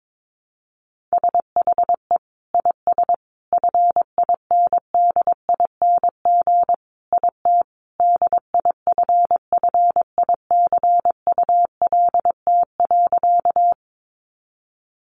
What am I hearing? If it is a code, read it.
Morse code, heard as a keyed tone, about 22 wpm, 700 Hz. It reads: SHE IS FINDING IT DIFFICULT.
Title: Sentences from Top 1100 Words 22wpm